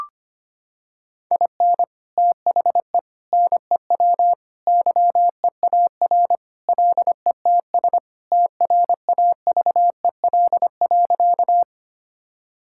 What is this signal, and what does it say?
Morse code, heard as a keyed tone, about 25 wpm, 700 Hz. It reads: IN THE NEW YEAR LETS TRAVEL.
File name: practice / 25wpm / Sentences with 3-Word Phrases - 25wpm